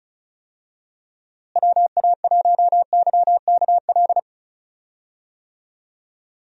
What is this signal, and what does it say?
Morse code, heard as a keyed tone, about 36 wpm, 700 Hz. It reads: WA1YKL
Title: USA Call Signs - 6 Characters - Repeated 2x - 35wpm